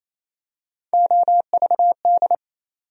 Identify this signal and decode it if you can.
Morse code, heard as a keyed tone, about 28 wpm, 700 Hz. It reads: OVD
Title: USA Call Sign Postfixes - 3 Letters 28wpm